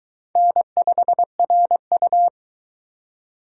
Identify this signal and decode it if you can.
Morse code, heard as a keyed tone, about 26 wpm, 700 Hz. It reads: N5RU